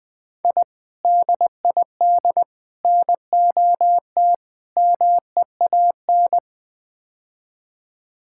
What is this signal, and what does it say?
Morse code, heard as a keyed tone, about 20 wpm, 700 Hz. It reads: I DID NOT MEAN
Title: Build A Phrase - 5-Word Phrases - 20wpm